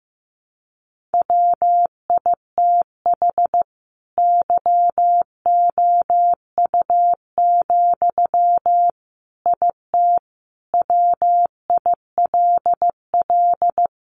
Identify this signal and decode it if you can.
Morse code, heard as a keyed tone, about 15 wpm, 700 Hz. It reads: WITH YOU, IT WILL
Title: Sentences from Top 1400 Words 15wpm